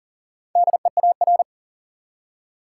Morse code, heard as a keyed tone, about 40 wpm, 700 Hz. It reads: DEAR
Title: Four Letter Words - 40wpm